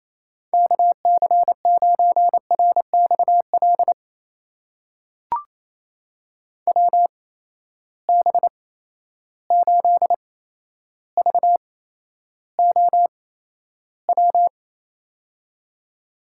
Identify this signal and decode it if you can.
Morse code, heard as a keyed tone, about 28 wpm, spaced as for 12 wpm, 700 Hz. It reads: KC9RXL WB8VOW
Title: USA Call Signs - 6 Characters - 8x Character Spacing - 28wpm